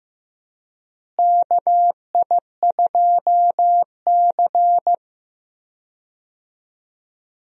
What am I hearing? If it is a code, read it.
Morse code, heard as a keyed tone, about 15 wpm, 700 Hz. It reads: KI2C
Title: USA Call Signs - 4 Characters - Repeated 2x - 15wpm